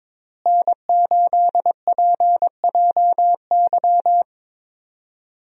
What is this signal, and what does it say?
Morse code, heard as a keyed tone, about 22 wpm, 700 Hz. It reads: N8PJY
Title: USA Call Signs - 5 Characters - Repeated 3x - 22wpm